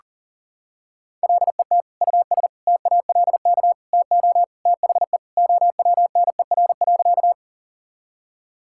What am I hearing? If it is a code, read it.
Morse code, heard as a keyed tone, about 40 wpm, 700 Hz. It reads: LET US TALK TO THE OWNER.
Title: Sentences from Top 1600 Words 40wpm